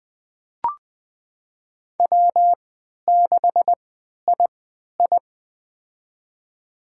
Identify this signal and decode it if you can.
Morse code, heard as a keyed tone, about 20 wpm, spaced as for 11 wpm, 700 Hz. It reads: W6II
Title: USA Call Signs - 3x Character Spacing - 20wpm